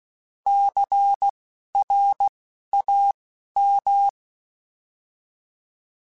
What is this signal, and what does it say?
Morse code, heard as a keyed tone, about 16 wpm, spaced as for 11 wpm, 800 Hz. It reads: CRAM
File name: Alphabetical / Alphabetical mp3s / Cs @10